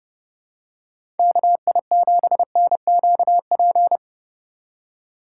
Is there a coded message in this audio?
Yes: Morse code from 1.2 to 4.0 s, heard as a keyed tone, about 30 wpm, 700 Hz. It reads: KI7NQP